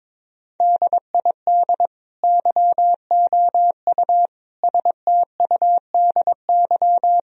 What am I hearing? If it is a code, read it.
Morse code, heard as a keyed tone, about 22 wpm, 700 Hz. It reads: DID YOU STUDY
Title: Sentences from Top 1000 Words - Review Part 4 - 22wpm